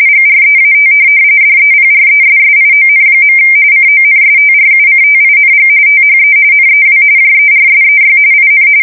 RTTY, about 45.45 baud. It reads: GREAT DEPRESSION. CONSTRUCTING THE NEW JERSEY IN
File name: fsk170 / USS-New-Jersey